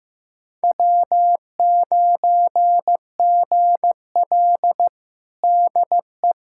Morse code, heard as a keyed tone, about 15 wpm, 700 Hz. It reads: W9GL DE
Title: Practice QSO - Exchange 4 - Ending - 15wpm